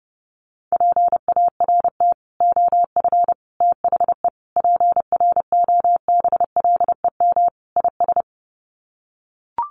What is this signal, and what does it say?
Morse code, heard as a keyed tone, about 30 wpm, 700 Hz. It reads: PART OF THE PROBLEM IS
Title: Sets of 5 Words Encore2 30wpm